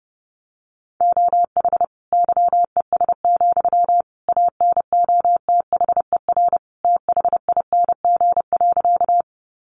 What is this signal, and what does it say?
Morse code, heard as a keyed tone, about 28 wpm, 700 Hz. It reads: OH YES, ANOTHER THING.
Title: Sentences from Top 600 Words 30wpm